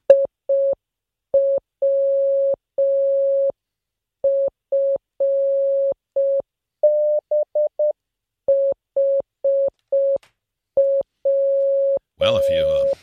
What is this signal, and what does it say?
Morse code, heard as a keyed tone, about 5 wpm, 550 Hz, with speech. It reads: IWF HW